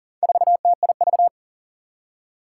Morse code, heard as a keyed tone, about 40 wpm, 700 Hz. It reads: 4TIV
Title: Four Challenging Letter-Number Combinations 40wpm